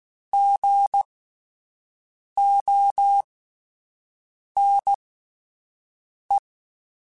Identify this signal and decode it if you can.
Morse code, heard as a keyed tone, about 16 wpm, spaced as for 5 wpm, 800 Hz. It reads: GONE
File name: Spaced Alphas / Spaced Alphas mp3s / Gs sp @10